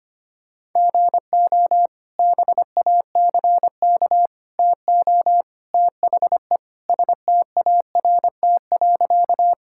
Morse code, heard as a keyed tone, about 25 wpm, 700 Hz. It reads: GO BACK TO THE START.
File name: practice / 25wpm / Sentences with 3-Word Phrases - 25wpm